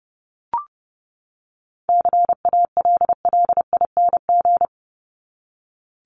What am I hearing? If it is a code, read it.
Morse code, heard as a keyed tone, about 30 wpm, 700 Hz. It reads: CALLING
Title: ING Suffix - Top 100 Words with Base Words 30wpm